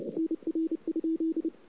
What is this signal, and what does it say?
Morse code, heard as a keyed tone, about 29 wpm, 320 Hz. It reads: NR?
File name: CW / 2008-ss-cw-k5tr-op-n5rz -06:00